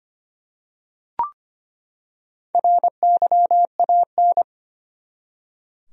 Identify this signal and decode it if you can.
Morse code, heard as a keyed tone, about 25 wpm, 700 Hz. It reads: RYAN
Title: USA Common Names - Max 4 Letters - 25wpm